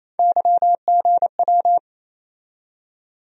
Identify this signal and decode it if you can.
Morse code, heard as a keyed tone, about 28 wpm, 700 Hz. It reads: YGW
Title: USA Call Sign Postfixes - 3 Letters 28wpm